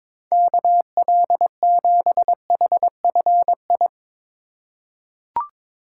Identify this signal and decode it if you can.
Morse code, heard as a keyed tone, about 22 wpm, 700 Hz. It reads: KL7HFI
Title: K1USN SST QSO Practice - 22wpm